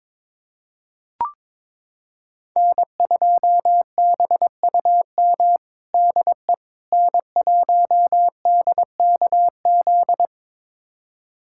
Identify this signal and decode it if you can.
Morse code, heard as a keyed tone, about 22 wpm, 700 Hz. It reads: N2BUM DE N1DKZ
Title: USA Callsign DE Callsign - 5 Characters - 22wpm